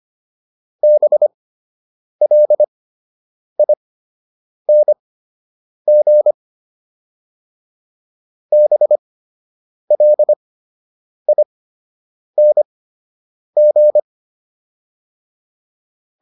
Morse code, heard as a keyed tone, about 25 wpm, spaced as for 8 wpm, 600 Hz. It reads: BLING BLING